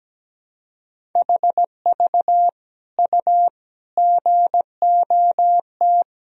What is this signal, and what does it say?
Morse code, heard as a keyed tone, about 17 wpm, 700 Hz. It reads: HV U GOT
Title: Sentences from Top 300 Words and Abbreviations 17wpm